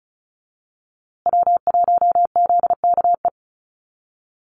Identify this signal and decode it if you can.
Morse code, heard as a keyed tone, about 34 wpm, 700 Hz. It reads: W1ZKE
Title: USA Call Signs Repeated 3x 35wpm